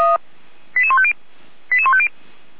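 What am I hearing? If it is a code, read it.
Five-tone selective call (ZVEI), heard as quick runs of five tones, tones of about 70 ms. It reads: 88150 88150
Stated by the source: ZVEI five-tone selective call received with an SDR.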